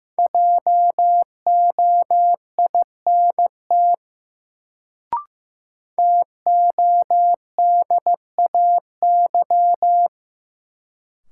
Morse code, heard as a keyed tone, about 15 wpm, 700 Hz. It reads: JOINT TODAY
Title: Sets of 5 Words Encore - New Word Drill - 15wpm